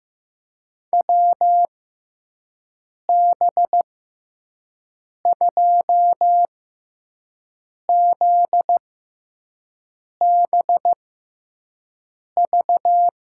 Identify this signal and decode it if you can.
Morse code, heard as a keyed tone, about 15 wpm, spaced as for 5 wpm, 700 Hz. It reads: WB2ZBV
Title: USA Call Signs - 6 Characters - 6x Character Spacing - 15wpm